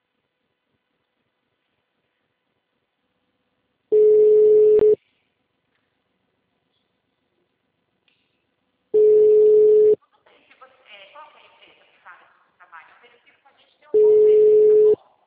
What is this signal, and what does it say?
Ringback tone (Europe, 425 Hz), with speech.
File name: out- -2003- - - .7256